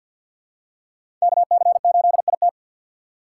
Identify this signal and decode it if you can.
Morse code, heard as a keyed tone, about 51 wpm, 700 Hz. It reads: KC8IT
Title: USA Call Signs - 3x Character Spacing - 50wpm